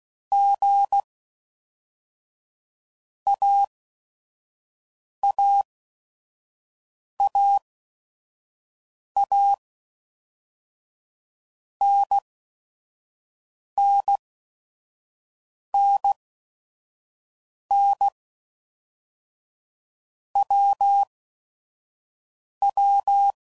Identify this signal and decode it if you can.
Morse code, heard as a keyed tone, about 16 wpm, 800 Hz. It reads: G A A A A N N N N W W
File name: Basic Letter Groups By Blocks / Groups@5 / AWUNDG@5 f15